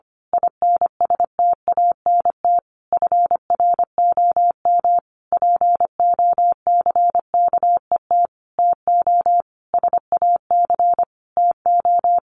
Morse code, heard as a keyed tone, about 25 wpm, 700 Hz. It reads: INSTANT FROM POCKET TO SAC TO
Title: A Sound Of Thunder 25wpm